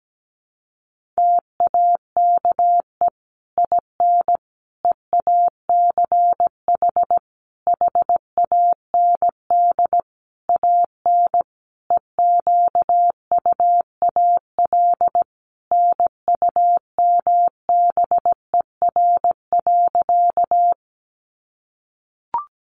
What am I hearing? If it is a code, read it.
Morse code, heard as a keyed tone, about 17 wpm, 700 Hz. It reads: TAKE IN EACH HAND AN EQUAL NUMBER.
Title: Sentences from Top 1400 Words 17wpm